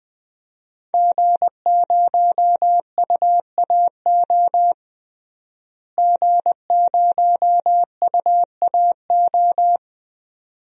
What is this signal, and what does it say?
Morse code, heard as a keyed tone, about 20 wpm, 700 Hz. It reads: G0UAO G0UAO
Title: DXCC-100 Call Signs - 5 Characters - Repeated 2x - 20wpm